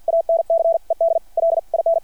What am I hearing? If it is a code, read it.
Morse code, heard as a keyed tone, about 53 wpm, spaced as for 46 wpm, 660 Hz. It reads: ANCED LIN
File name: archived / CRSnet